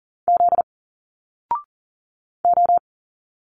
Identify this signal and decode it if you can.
Morse code, heard as a keyed tone, about 40 wpm, 700 Hz. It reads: Z O
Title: Single Letters 40wpm